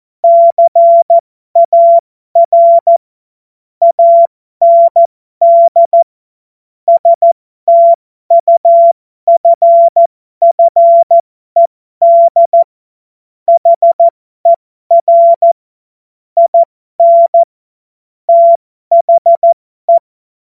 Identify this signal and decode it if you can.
Morse code, heard as a keyed tone, about 14 wpm, spaced as for 12 wpm, 680 Hz. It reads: CAR AND STUFFED HER IN THE